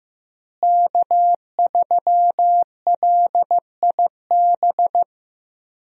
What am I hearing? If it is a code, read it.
Morse code, heard as a keyed tone, about 15 wpm, 700 Hz. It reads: K3LIB
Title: USA Call Signs - 5 Characters - Repeated 3x - 15wpm